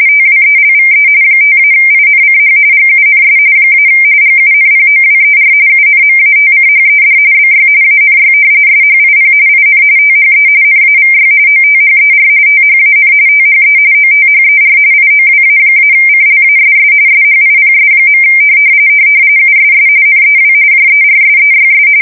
RTTY, about 45.45 baud. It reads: TWELVE 5-INCH GUNS - GUNPOWER UNMATCHED BY ANY SHIP IN THE NAVY OTHER THAN THE OTHER IOWA-CLASS SHIPS. USS NEW JERSEYS
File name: fsk170 / battleshipnj